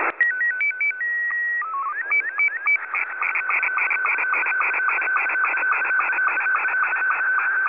SSTV (slow-scan television), Scottie 2.
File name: SSTV images / websdr recording start 14230.0kHz